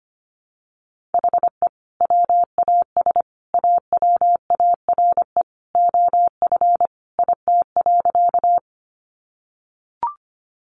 Morse code, heard as a keyed tone, about 25 wpm, 700 Hz. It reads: HE WAS AWARE OF IT.
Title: Sentences from Top 1200 Words 25wpm